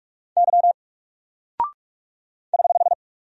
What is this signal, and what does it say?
Morse code, heard as a keyed tone, about 45 wpm, 700 Hz. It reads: Y <HH>
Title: Single Character (Letter:Number:Punctuation:Prosign) 45wpm